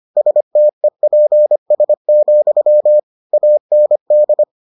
Morse code, heard as a keyed tone, about 25 wpm, 600 Hz. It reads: STEPS, AND